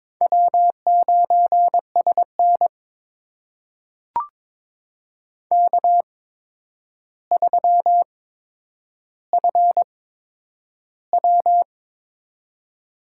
Morse code, heard as a keyed tone, about 22 wpm, spaced as for 9 wpm, 700 Hz. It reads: W9SN K3FW
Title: USA Call Signs - 4 Characters - 8x Character Spacing - 22wpm